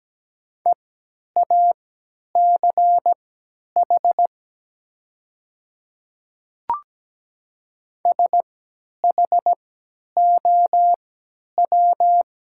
Morse code, heard as a keyed tone, about 17 wpm, spaced as for 10 wpm, 690 Hz. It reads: EACH SHOW
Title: 3x Character Spacing - Top 200 Words - 17wpm